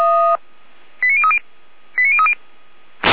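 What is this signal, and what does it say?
Five-tone selective call (ZVEI), heard as quick runs of five tones, tones of about 70 ms. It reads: 89030 89030
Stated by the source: ZVEI five-tone selective call received on an SDR.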